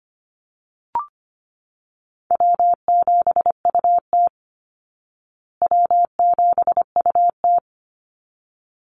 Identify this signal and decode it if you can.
Morse code, heard as a keyed tone, about 25 wpm, 700 Hz. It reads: W7UT W7UT
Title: Easy Call Signs Repeated 2x - 25wpm